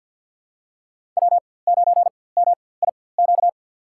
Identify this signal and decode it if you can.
Morse code, heard as a keyed tone, about 46 wpm, spaced as for 23 wpm, 700 Hz. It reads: W9MIQ
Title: USA Call Signs - 5 Characters - 4x Character Spacing - 50wpm